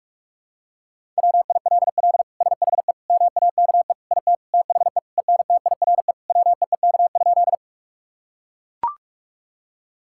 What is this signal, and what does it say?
Morse code, heard as a keyed tone, about 44 wpm, 700 Hz. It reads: WILL SHE MAKE IT THE ENTIRE WEEK?
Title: Sentences from Top 1000 Words - Review Part 5 - 45wpm